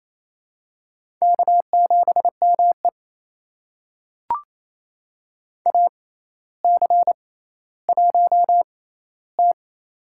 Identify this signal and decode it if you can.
Morse code, heard as a keyed tone, about 28 wpm, spaced as for 14 wpm, 700 Hz. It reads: K7ME AC1T
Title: USA Call Signs - 4 Characters - 6x Character Spacing - 28wpm